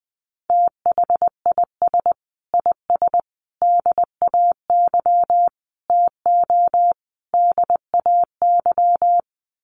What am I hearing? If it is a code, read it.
Morse code, heard as a keyed tone, about 20 wpm, 700 Hz. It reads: THIS IS DAY TO DAY
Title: Sets of 5 Words Encore 20wpm